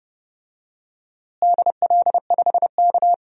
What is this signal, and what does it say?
Morse code, heard as a keyed tone, about 30 wpm, 700 Hz. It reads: DL5K
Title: DXCC-100 Call Signs - 4 Characters - Repeated 2x - 30wpm